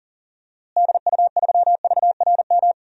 Morse code, heard as a keyed tone, about 40 wpm, 700 Hz. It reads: DU3VRM
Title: DXCC-100 Call Signs - 6 Characters - Repeated 3x - 40wpm